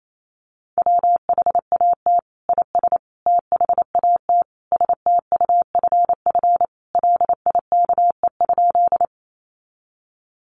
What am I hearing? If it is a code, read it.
Morse code, heard as a keyed tone, about 28 wpm, 700 Hz. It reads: WHAT IS THAT STUFF LIKE?
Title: Sentences from Top 1400 Words 28wpm